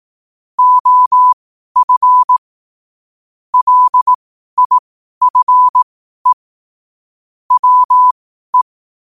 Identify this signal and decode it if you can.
Morse code, heard as a keyed tone, about 18 wpm, spaced as for 12 wpm, 1000 Hz. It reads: OF LIFE WE
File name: qotdfiles / 12wpm